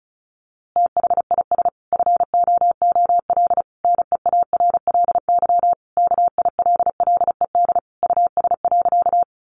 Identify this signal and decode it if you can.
Morse code, heard as a keyed tone, about 34 wpm, 700 Hz. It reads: THIS FOOL NEARLY KILLED US.
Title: A Sound Of Thunder 35wpm